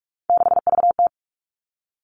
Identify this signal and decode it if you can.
Morse code, heard as a keyed tone, about 46 wpm, 700 Hz. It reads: 6VT
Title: Three Challenging Letter-Number Combinations 45wpm